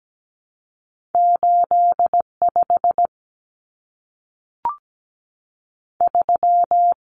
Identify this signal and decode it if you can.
Morse code, heard as a keyed tone, about 17 wpm, 700 Hz. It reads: 85 3
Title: Common Cut Numbers - 2-Digit 17wpm